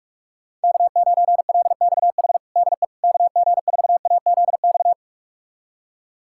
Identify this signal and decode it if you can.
Morse code, heard as a keyed tone, about 46 wpm, 700 Hz. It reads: K9LXH DE KG4AZX
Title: USA Callsign DE Callsign 45wpm